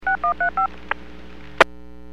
DTMF, heard as telephone touch tones, tones of about 110 ms. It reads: 6135
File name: WPS / 00880 19-35 WPS 451.25 WPS